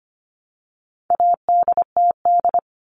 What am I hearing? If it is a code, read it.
Morse code, heard as a keyed tone, about 25 wpm, 700 Hz. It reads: ADTD
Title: All Cut Numbers - 4-Digit 25wpm